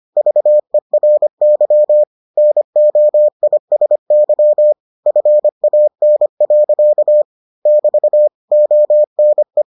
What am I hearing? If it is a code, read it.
Morse code, heard as a keyed tone, about 25 wpm, 600 Hz. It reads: VERY NOISY FAN. <BT> ONE